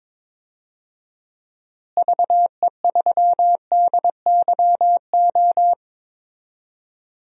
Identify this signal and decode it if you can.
Morse code, heard as a keyed tone, about 22 wpm, 700 Hz. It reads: VE3DYO
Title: DXCC-100 Call Signs - 6 Characters - Repeated 3x - 22wpm